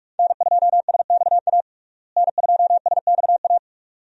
Morse code, heard as a keyed tone, about 46 wpm, 700 Hz. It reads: N2SXA N2SXA
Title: USA Call Signs - 5 Characters - Repeated 2x - 45wpm